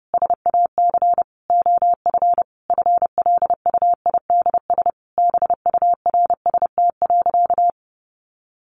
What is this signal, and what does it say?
Morse code, heard as a keyed tone, about 30 wpm, 700 Hz. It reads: SAC OF FLUIDS BURST.
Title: A Sound Of Thunder 30wpm